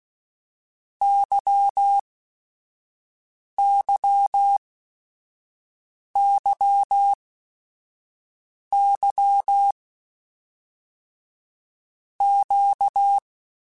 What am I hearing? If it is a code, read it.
Morse code, heard as a keyed tone, about 16 wpm, 800 Hz. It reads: Y Y Y Y Q
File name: Basic Sets / Basic Letter Groups Collective7 / LFCYQ@5